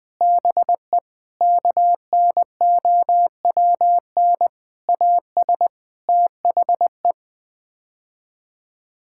Morse code, heard as a keyed tone, about 20 wpm, 700 Hz. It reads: BE KNOWN AS THE
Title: Sets of 8 Words - 20wpm